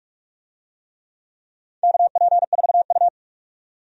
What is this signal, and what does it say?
Morse code, heard as a keyed tone, about 45 wpm, 700 Hz. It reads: KP4U
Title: DXCC-100 Call Signs - 4 Characters - 45wpm